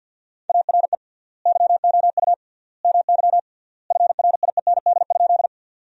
Morse code, heard as a keyed tone, about 50 wpm, 700 Hz. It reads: ARE YOU MY FRIEND?
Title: 3x Word-Spacing - Sentences from Top 300 Words - 50wpm